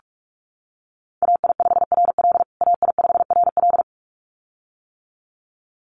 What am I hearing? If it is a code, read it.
Morse code, heard as a keyed tone, about 44 wpm, 700 Hz. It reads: AI5RL AI5RL
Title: USA Call Signs Repeated 2x 45wpm